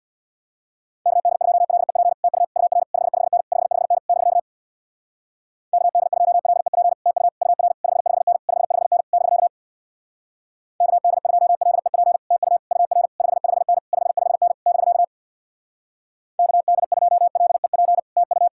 Morse code, heard as a keyed tone, about 64 wpm, 700 Hz. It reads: KD2LEP TU UR 55N 55N <BK> KD2LEP TU UR 55N 55N <BK> KD2LEP TU UR 55N 55N <BK> KD2LEP TU